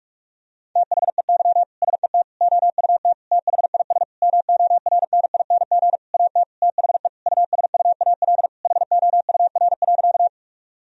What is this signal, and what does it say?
Morse code, heard as a keyed tone, about 46 wpm, 700 Hz. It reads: THEY SET OUT THIS MORNING AT THE USUAL HOUR.
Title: Sentences from Top 1000 Words - Review Part 2 - 45wpm